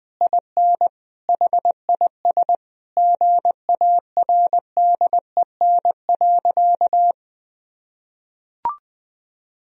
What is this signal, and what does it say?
Morse code, heard as a keyed tone, about 20 wpm, 700 Hz. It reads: IN HIS GARDEN.
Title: Sentences from Top 1000 Words - Review Part 4 - 20wpm